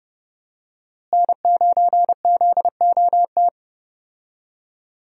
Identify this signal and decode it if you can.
Morse code, heard as a keyed tone, about 31 wpm, 700 Hz. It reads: N9ZOT
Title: USA Call Signs - 5 Characters - Repeated 2x - 30wpm